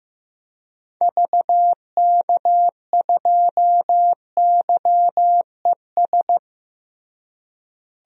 Morse code, heard as a keyed tone, about 15 wpm, 700 Hz. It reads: VK2YES
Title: DXCC-100 Call Signs - 6 Characters - Repeated 2x - 15wpm